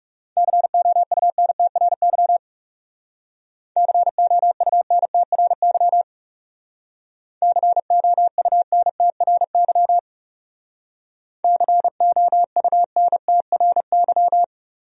Morse code, heard as a keyed tone, about 37 wpm, 700 Hz. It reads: COUNTRY COUNTRY COUNTRY COUNTRY
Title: Speed-Racing Top 300 Words 30wpm